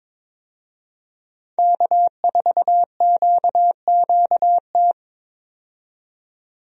Morse code, heard as a keyed tone, about 22 wpm, 700 Hz. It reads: K4QQT
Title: USA Call Signs - 5 Characters - 22wpm